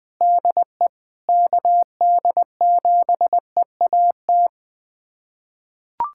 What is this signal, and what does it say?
Morse code, heard as a keyed tone, about 20 wpm, 700 Hz. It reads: DE KD7EAT
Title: USA Callsign DE Callsign - 6 Characters - 20wpm